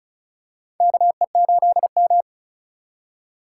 Morse code, heard as a keyed tone, about 34 wpm, 700 Hz. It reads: KE8M
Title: K1USN SST QSO Practice - 35wpm